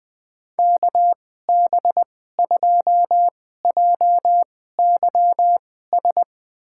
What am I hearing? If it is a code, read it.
Morse code, heard as a keyed tone, about 20 wpm, spaced as for 14 wpm, 700 Hz. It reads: KB2JYS
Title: USA Call Signs - 6 Characters - 2x Character Spacing - 20wpm